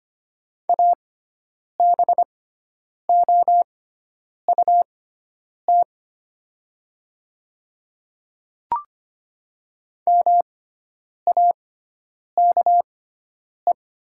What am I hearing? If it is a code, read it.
Morse code, heard as a keyed tone, about 25 wpm, spaced as for 9 wpm, 700 Hz. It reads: ABOUT MAKE
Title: 6x Character Spacing - Top 200 Words - 25wpm